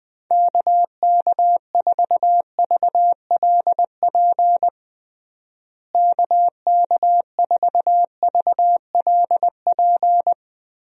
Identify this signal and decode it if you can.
Morse code, heard as a keyed tone, about 20 wpm, 700 Hz. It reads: KK4VLP KK4VLP
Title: USA Call Signs - 6 Characters - Repeated 3x - 20wpm